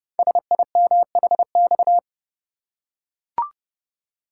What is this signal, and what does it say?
Morse code, heard as a keyed tone, about 28 wpm, 700 Hz. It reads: SIMHX
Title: Code Groups - 5 Letters - 30wpm